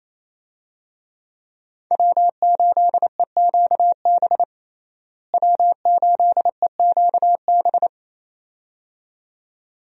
Morse code, heard as a keyed tone, about 28 wpm, 700 Hz. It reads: W8EQB W8EQB
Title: USA Call Signs - 5 Characters - Repeated 2x - 28wpm